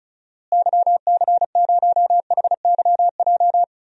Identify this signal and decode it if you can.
Morse code, heard as a keyed tone, about 34 wpm, 700 Hz. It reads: YC0HYJ
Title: DXCC-100 Call Signs - 6 Characters - Repeated 2x - 35wpm